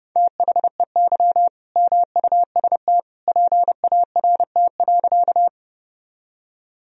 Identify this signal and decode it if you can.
Morse code, heard as a keyed tone, about 30 wpm, 700 Hz. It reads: THEY MUST PART.
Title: Sentences from Top 1000 Words - Review Part 1 - 30wpm